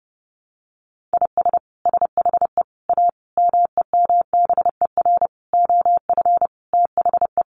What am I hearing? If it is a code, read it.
Morse code, heard as a keyed tone, about 28 wpm, 700 Hz. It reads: IS SHE A MEMBER OF THE
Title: Sentences from Top 1100 Words 30wpm